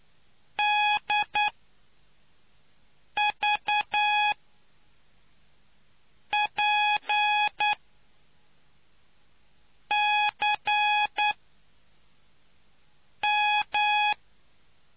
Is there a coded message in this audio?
Yes: Morse code from 0.6 to 14.1 s, heard as a keyed tone, about 10 wpm, spaced as for 4 wpm, 840 Hz. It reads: DVPCM